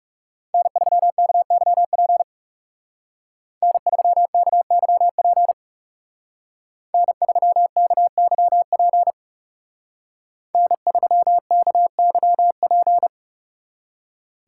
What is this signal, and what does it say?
Morse code, heard as a keyed tone, about 37 wpm, 700 Hz. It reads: N3KYP N3KYP N3KYP N3KYP
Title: Speed Racing - USA 1x3 Call Signs 30wpm